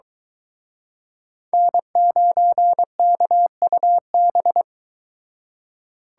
Morse code, heard as a keyed tone, about 23 wpm, 700 Hz. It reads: N9KUB